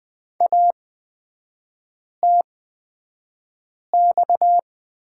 Morse code, heard as a keyed tone, about 20 wpm, spaced as for 5 wpm, 700 Hz. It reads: ATX
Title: 243 Lesson 39e - Letter z - Three Characters 20wpm